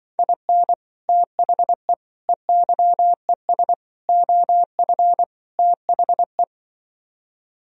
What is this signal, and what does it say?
Morse code, heard as a keyed tone, about 24 wpm, 700 Hz. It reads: IN THE EYES OF THE
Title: Speed-Racing - Sets of 5 Words 20wpm